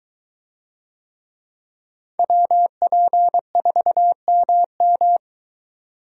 Morse code, heard as a keyed tone, about 22 wpm, 700 Hz. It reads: WP4MM